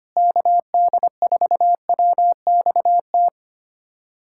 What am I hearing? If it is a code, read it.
Morse code, heard as a keyed tone, about 25 wpm, 700 Hz. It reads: KD4WXT
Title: USA Callsign DE Callsign - 6 Characters - 25wpm